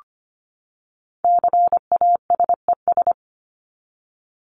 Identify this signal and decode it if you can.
Morse code, heard as a keyed tone, about 25 wpm, 700 Hz. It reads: CASES
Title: ES:S Suffix - Top 100 Words with Base Words 25wpm